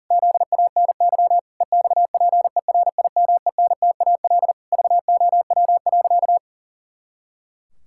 Morse code, heard as a keyed tone, about 37 wpm, 700 Hz. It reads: ZANY EXPERIMENTAL VOW.
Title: Pangram Challenge v2 - 40wpm